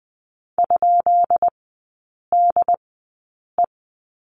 Morse code, heard as a keyed tone, about 20 wpm, spaced as for 8 wpm, 700 Hz. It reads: ?DE
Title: 138 Lesson 23e - Letter f - Three Characters 20wpm